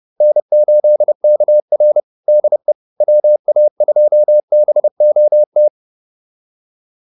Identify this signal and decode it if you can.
Morse code, heard as a keyed tone, about 30 wpm, 600 Hz. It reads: N8KR DE WA2BOT